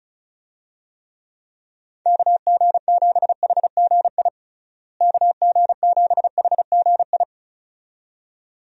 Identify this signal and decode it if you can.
Morse code, heard as a keyed tone, about 34 wpm, 700 Hz. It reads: KG7HGI KG7HGI
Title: USA Call Signs - 6 Characters - Repeated 2x - 35wpm